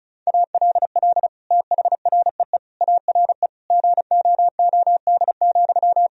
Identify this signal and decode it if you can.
Morse code, heard as a keyed tone, about 36 wpm, 700 Hz. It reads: ALL THREE ARE GOOD,
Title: Sentences from Top 1000 Words - Review Part 4 - 35wpm